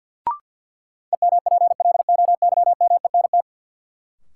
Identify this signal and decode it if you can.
Morse code, heard as a keyed tone, about 51 wpm, 700 Hz. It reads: EMPLOYMENT
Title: Sentences from Top 1500 Words - New Word Drill Alt - 50wpm